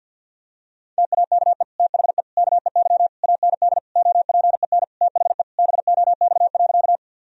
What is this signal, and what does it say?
Morse code, heard as a keyed tone, about 50 wpm, 700 Hz. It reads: TAKE THE KEY AND OPEN THE BOX.
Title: Sentences from Top 1000 Words - Review Part 4 - 50wpm